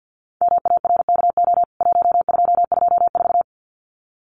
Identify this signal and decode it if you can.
Morse code, heard as a keyed tone, about 49 wpm, 700 Hz. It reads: MARKO 1224
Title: CWT Exchange 50wpm